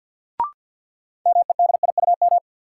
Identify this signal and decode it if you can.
Morse code, heard as a keyed tone, about 52 wpm, 700 Hz. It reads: MEDIUM
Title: Sentences from Top 900 Words - New Word Drill - 50wpm